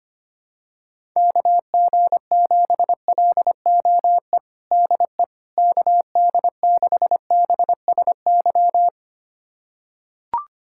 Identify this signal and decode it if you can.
Morse code, heard as a keyed tone, about 25 wpm, 700 Hz. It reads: KG7LOE DE KD6BSY
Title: USA Callsign DE Callsign - 6 Characters - 25wpm